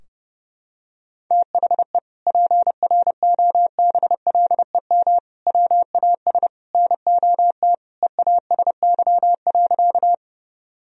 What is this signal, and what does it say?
Morse code, heard as a keyed tone, about 30 wpm, 700 Hz. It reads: THE PROBLEM WAS NOT EASY.
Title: Sentences from Top 1000 Words - Review Part 2 - 30wpm